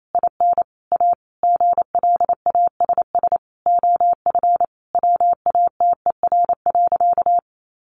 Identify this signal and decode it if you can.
Morse code, heard as a keyed tone, about 28 wpm, 700 Hz. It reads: IN A GLASS OF WATER.
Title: Sentences from Top 1300 Words 28wpm